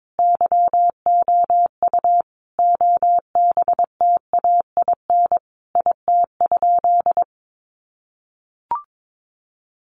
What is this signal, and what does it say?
Morse code, heard as a keyed tone, about 22 wpm, 700 Hz. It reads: YOU OBTAIN IT?
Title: Sentences from Top 1300 Words 22wpm